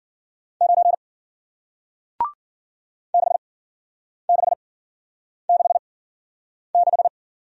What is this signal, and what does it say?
Morse code, heard as a keyed tone, about 52 wpm, spaced as for 9 wpm, 700 Hz. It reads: 9 6666